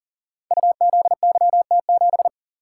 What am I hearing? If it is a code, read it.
Morse code, heard as a keyed tone, about 40 wpm, 700 Hz. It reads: UZYT7
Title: Code Groups - 5 Letter:Numbers - 40wpm